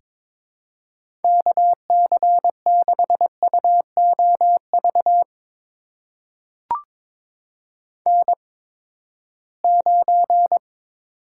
Morse code, heard as a keyed tone, about 22 wpm, spaced as for 15 wpm, 700 Hz. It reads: KC6UOV N9
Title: USA Call Signs - 8x Character Spacing - 22wpm